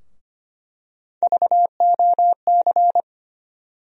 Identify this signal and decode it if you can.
Morse code, heard as a keyed tone, about 25 wpm, 700 Hz. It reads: VOC
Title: USA Call Sign Postfixes - 3 Letters 25wpm